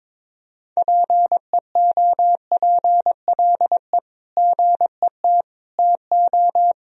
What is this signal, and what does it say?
Morse code, heard as a keyed tone, about 22 wpm, 700 Hz. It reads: PEOPLE GET TO
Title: Sentences from Top 1000 Words - Review Part 1 - 22wpm